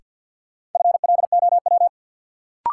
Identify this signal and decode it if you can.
Morse code, heard as a keyed tone, about 47 wpm, 700 Hz. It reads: FLOW
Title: Sentences from Top 1200 Words - New Word Drill - 50wpm